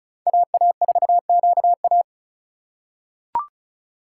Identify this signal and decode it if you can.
Morse code, heard as a keyed tone, about 36 wpm, 700 Hz. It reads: AA4QA
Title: USA Call Signs - 2x Character Spacing - 35wpm